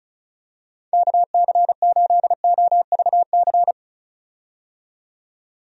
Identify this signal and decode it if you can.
Morse code, heard as a keyed tone, about 36 wpm, 700 Hz. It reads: KC8OVC
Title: USA Call Signs - 6 Characters - 35wpm